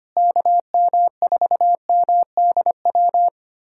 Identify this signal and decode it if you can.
Morse code, heard as a keyed tone, about 25 wpm, 700 Hz. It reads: KM4MDW
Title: USA Callsign DE Callsign 25wpm